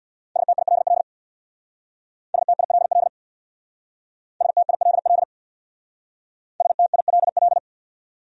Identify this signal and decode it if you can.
Morse code, heard as a keyed tone, about 63 wpm, 700 Hz. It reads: STILL STILL STILL STILL